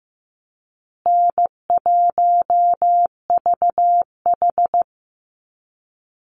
Morse code, heard as a keyed tone, about 15 wpm, 700 Hz. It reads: N1VH
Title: Call Signs Repeated 2x - 15wpm